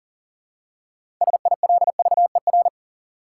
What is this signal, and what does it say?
Morse code, heard as a keyed tone, about 40 wpm, 700 Hz. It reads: SILVER
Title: Six Letter Words - 40wpm